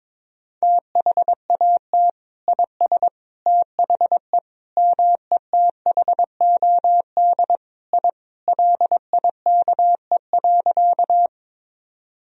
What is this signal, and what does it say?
Morse code, heard as a keyed tone, about 22 wpm, 700 Hz. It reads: THAT IS THE METHOD I LIKE.
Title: Sentences from Top 1000 Words - Review Part 3 - 22wpm